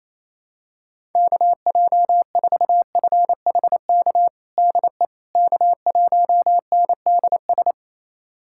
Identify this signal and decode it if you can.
Morse code, heard as a keyed tone, about 28 wpm, 700 Hz. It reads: KJ4FHK DE K1NDS
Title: USA Callsign DE Callsign 28wpm